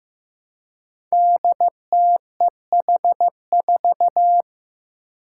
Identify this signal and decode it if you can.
Morse code, heard as a keyed tone, about 15 wpm, 700 Hz. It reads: DTEH4
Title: Five Challenging Letter-Number Combinations 15wpm